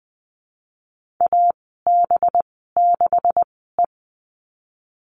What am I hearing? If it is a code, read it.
Morse code, heard as a keyed tone, about 20 wpm, spaced as for 15 wpm, 700 Hz. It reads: AB6E
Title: 186 Lesson 30g - Letter b - Call Signs 20wpm